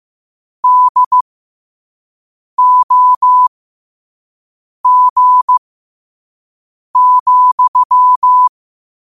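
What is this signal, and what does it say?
Morse code, heard as a keyed tone, about 15 wpm, spaced as for 5 wpm, 1000 Hz. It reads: DOG,
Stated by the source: Morse code practice at 5 wpm.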